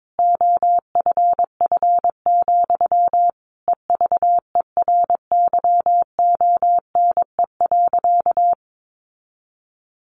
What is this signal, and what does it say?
Morse code, heard as a keyed tone, about 22 wpm, 700 Hz. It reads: OFF, EVERYONE.
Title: A Sound Of Thunder 22wpm